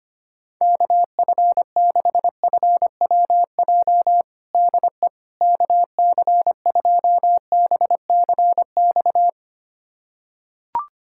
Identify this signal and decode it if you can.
Morse code, heard as a keyed tone, about 25 wpm, 700 Hz. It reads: KF6FWJ DE KC2BCX
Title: USA Callsign DE Callsign - 6 Characters - 25wpm